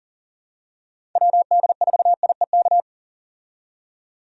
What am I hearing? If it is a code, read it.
Morse code, heard as a keyed tone, about 40 wpm, 700 Hz. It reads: WD4IEK
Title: USA Call Signs - 6 Characters - 40wpm